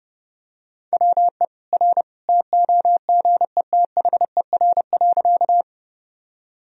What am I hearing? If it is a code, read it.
Morse code, heard as a keyed tone, about 30 wpm, 700 Hz. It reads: WE R TOGETHER.
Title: Sentences from Top 300 Words and Abbreviations 30wpm